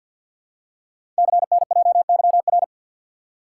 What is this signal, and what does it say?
Morse code, heard as a keyed tone, about 46 wpm, 700 Hz. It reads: CNJYR
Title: Code Groups - 5 Letters - 50wpm